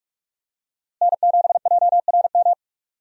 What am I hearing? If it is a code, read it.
Morse code, heard as a keyed tone, about 44 wpm, 700 Hz. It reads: N7JRM
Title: USA Call Signs - 5 Characters - Repeated 2x - 45wpm